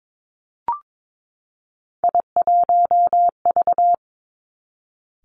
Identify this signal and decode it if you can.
Morse code, heard as a keyed tone, about 22 wpm, 700 Hz. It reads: I1V
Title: Three Challenging Letter-Number Combinations 22wpm